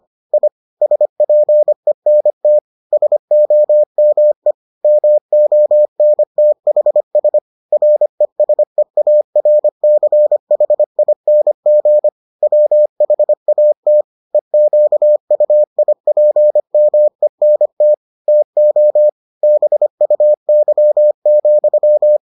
Morse code, heard as a keyed tone, about 25 wpm, 600 Hz. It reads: I SPENT SOME MONTHS RESEARCHING WHAT EQUIPMENT TO BUY,